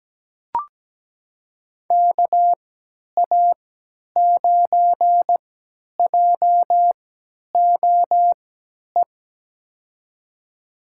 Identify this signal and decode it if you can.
Morse code, heard as a keyed tone, about 17 wpm, spaced as for 10 wpm, 700 Hz. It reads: KA9JOE
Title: USA Call Signs - 6 Characters - 3x Character Spacing - 17wpm